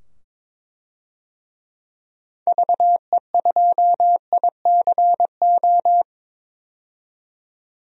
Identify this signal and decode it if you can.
Morse code, heard as a keyed tone, about 22 wpm, 700 Hz. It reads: VE2ICO
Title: DXCC-100 Call Signs - 6 Characters - 22wpm